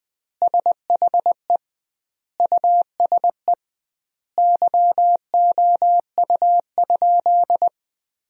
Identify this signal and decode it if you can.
Morse code, heard as a keyed tone, about 20 wpm, 700 Hz. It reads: SHE USE YOU?
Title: 2x Word-Spacing - Sentences from Top 200 Words - 20wpm